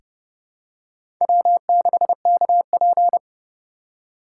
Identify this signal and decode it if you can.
Morse code, heard as a keyed tone, about 30 wpm, 700 Hz. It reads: W6KP
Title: USA Call Signs - 4 Characters - 4x Character Spacing - 30wpm